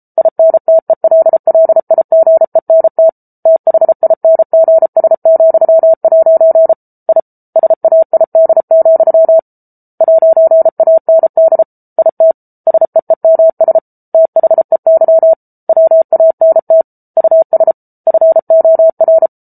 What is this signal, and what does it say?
Morse code, heard as a keyed tone, about 34 wpm, 650 Hz. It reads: INTELLIGENT THINGS,' I SAID, 'AND IT SEEMS THEY WANT US FOR